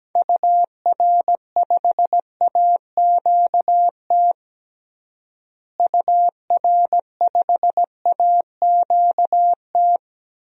Morse code, heard as a keyed tone, about 17 wpm, 700 Hz. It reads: UR5AQT UR5AQT
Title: DXCC-100 Call Signs - 6 Characters - Repeated 2x - 17wpm